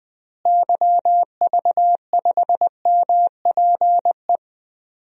Morse code, heard as a keyed tone, about 20 wpm, 700 Hz. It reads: YV5MPE
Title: DXCC-100 Call Signs - 6 Characters - Repeated 3x - 20wpm